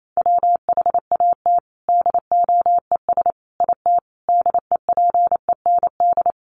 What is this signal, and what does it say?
Morse code, heard as a keyed tone, about 28 wpm, 700 Hz. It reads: WHAT DOES IT DEPEND
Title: Sentences from Top 1300 Words 28wpm